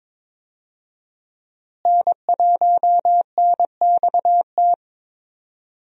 Morse code, heard as a keyed tone, about 22 wpm, 700 Hz. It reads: N1NXT
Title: USA Call Signs - 5 Characters - Repeated 2x - 22wpm